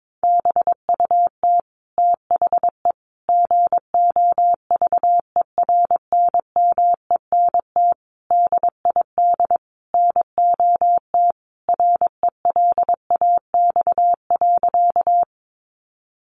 Morse code, heard as a keyed tone, about 22 wpm, 700 Hz. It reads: BUT THE GOVERNMENT DID NOT RELAX.
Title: Sentences from Top 1400 Words 22wpm